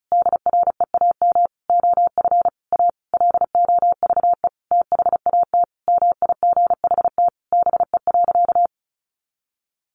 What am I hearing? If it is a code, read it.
Morse code, heard as a keyed tone, about 34 wpm, 700 Hz. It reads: DREAM OF A LOVE THAT MIGHT BE.
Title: Sentences from Top 1500 Words 35wpm